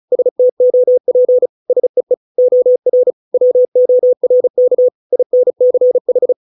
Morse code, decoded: STOP SEE OR WORK INCH

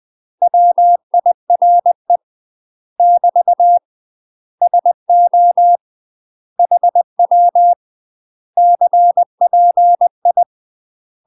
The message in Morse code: WIRE <BT> SO HW CPI

